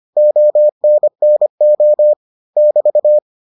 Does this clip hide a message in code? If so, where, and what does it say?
Yes: Morse code from 0.2 to 3.2 s, heard as a keyed tone, about 25 wpm, 600 Hz. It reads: ONNO <BT>